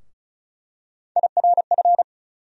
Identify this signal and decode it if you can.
Morse code, heard as a keyed tone, about 36 wpm, 700 Hz. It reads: IRF
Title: USA Call Sign Postfixes - 3 Letters 35wpm